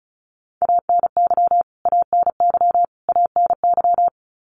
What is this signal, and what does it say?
Morse code, heard as a keyed tone, about 34 wpm, 700 Hz. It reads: ANY ANY ANY
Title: Top 300 Words Repeated 35wpm